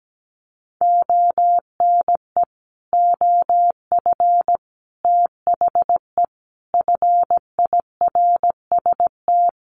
Morse code, heard as a keyed tone, about 17 wpm, 700 Hz. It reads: ONE OF THE FIRST
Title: Sets of 4 Words 17wpm